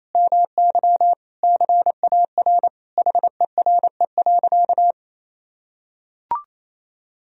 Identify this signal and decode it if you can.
Morse code, heard as a keyed tone, about 28 wpm, 700 Hz. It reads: MY CAR HERE.
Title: Sentences from Top 1000 Words - Review Part 1 - 28wpm